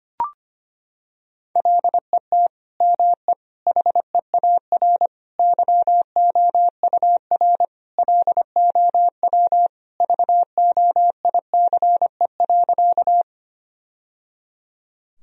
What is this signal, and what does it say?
Morse code, heard as a keyed tone, about 25 wpm, 700 Hz. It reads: LET ME HEAR YOUR LOW VOICE.
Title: Sentences from Top 1000 Words - Review Part 1 - 25wpm